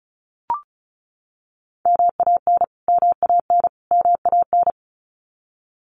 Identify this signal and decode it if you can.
Morse code, heard as a keyed tone, about 34 wpm, 700 Hz. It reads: MAN MAN MAN
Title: Top 200 Words Repeated 35wpm